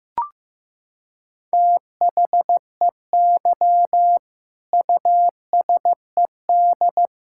Morse code, heard as a keyed tone, about 16 wpm, 700 Hz. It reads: THEY USED
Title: Sentences from Top 1000 Words - Review Part 6 - 15wpm